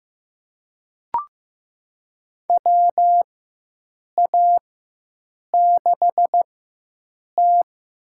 Morse code, heard as a keyed tone, about 15 wpm, spaced as for 7 wpm, 700 Hz. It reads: WA6T
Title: USA Call Signs - 4x Character Spacing - 15wpm